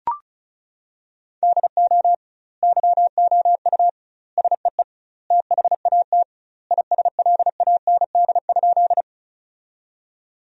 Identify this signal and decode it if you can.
Morse code, heard as a keyed tone, about 36 wpm, 700 Hz. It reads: DO YOU SEE THAT ISLAND?
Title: 2x Word-Spacing - Sentences from Top 500 Words - 35wpm